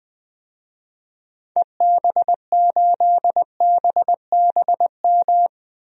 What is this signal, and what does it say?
Morse code, heard as a keyed tone, about 20 wpm, 700 Hz. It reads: EB8BBM
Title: DXCC-100 Call Signs - 6 Characters - Repeated 2x - 20wpm